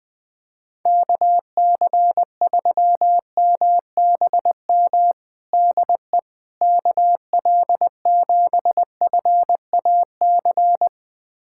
Morse code, heard as a keyed tone, about 20 wpm, 700 Hz. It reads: KC3MBM DE KL7FAC
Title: USA Callsign DE Callsign 20wpm